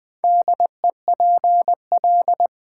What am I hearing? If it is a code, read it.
Morse code, heard as a keyed tone, about 20 wpm, 700 Hz. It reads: DEPL
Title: Build A Word - 4 Letters At A Time - 10-Letter Words - 20wpm